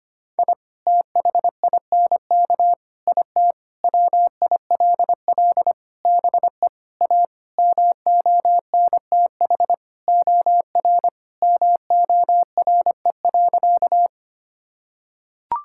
Morse code, heard as a keyed tone, about 25 wpm, 700 Hz. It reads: I THINK IT WILL BE A MONTH OR MORE.
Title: Sentences from Top 1000 Words - Review Part 5 - 25wpm